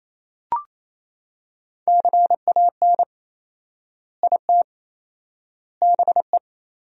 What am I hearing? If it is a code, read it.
Morse code, heard as a keyed tone, about 28 wpm, 700 Hz. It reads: CAN IT BE
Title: 4x Word-Spacing - Sentences from Top 500 Words - 28wpm